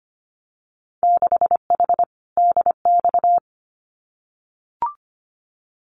Morse code, heard as a keyed tone, about 25 wpm, 700 Hz. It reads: 6H DX
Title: Winter Field Day Exchanges Encore 25wpm